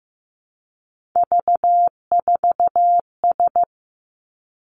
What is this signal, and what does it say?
Morse code, heard as a keyed tone, about 15 wpm, 700 Hz. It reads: V4S
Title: Three Challenging Letter-Number Combinations 15wpm